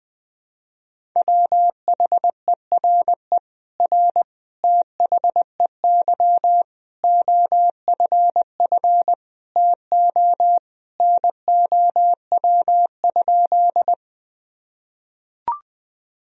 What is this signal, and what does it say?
Morse code, heard as a keyed tone, about 20 wpm, 700 Hz. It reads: WHERE R THEY OFF TO NOW?
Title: Sentences from Top 200 Words and Abbreviations 20wpm